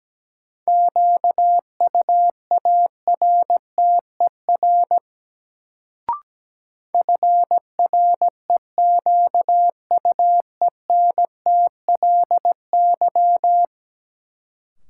Morse code, heard as a keyed tone, about 17 wpm, 700 Hz. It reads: QUARTER FREQUENTLY
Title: Sentences from Top 1300 Words - New Word Drill - 17wpm